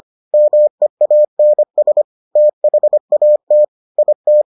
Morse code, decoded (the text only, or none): MEANS THAT IT